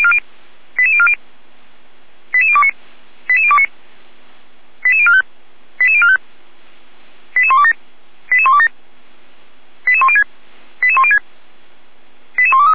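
Five-tone selective call (ZVEI), heard as quick runs of five tones, tones of about 70 ms. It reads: 88040 88029 88029 88045 88045 88127 88127 88106 88106 88123
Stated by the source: Five-tone ZVEI selective call 88191, received by SDR.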